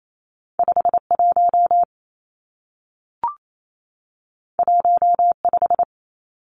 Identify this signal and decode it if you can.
Morse code, heard as a keyed tone, about 28 wpm, 700 Hz. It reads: 51 15
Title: Numbers - 2-Digit 28wpm